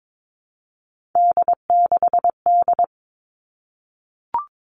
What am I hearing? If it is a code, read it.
Morse code, heard as a keyed tone, about 22 wpm, 700 Hz. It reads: D6D
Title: All Cut Numbers - 3-Digit 22wpm